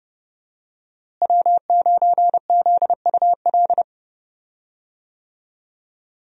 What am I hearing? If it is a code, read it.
Morse code, heard as a keyed tone, about 30 wpm, 700 Hz. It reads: W9ZUL
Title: USA Call Signs - 5 Characters - Repeated 3x - 30wpm